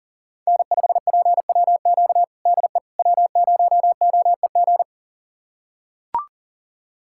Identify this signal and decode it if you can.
Morse code, heard as a keyed tone, about 40 wpm, 700 Hz. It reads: N5PWQ DE W0OEG
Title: USA Callsign DE Callsign - 5 Characters - 40wpm